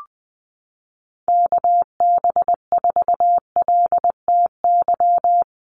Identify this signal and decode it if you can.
Morse code, heard as a keyed tone, about 19 wpm, 700 Hz. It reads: KB4LTY
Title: USA Call Signs Repeated 2x 20wpm